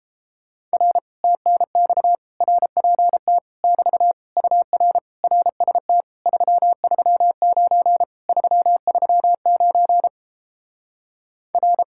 Morse code, heard as a keyed tone, about 31 wpm, 700 Hz. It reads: R TNX RPT <BT> UR RST 339 339 R